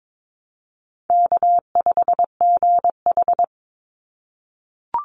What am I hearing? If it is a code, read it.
Morse code, heard as a keyed tone, about 22 wpm, 700 Hz. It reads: K5GH
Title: Easy Call Signs - 22wpm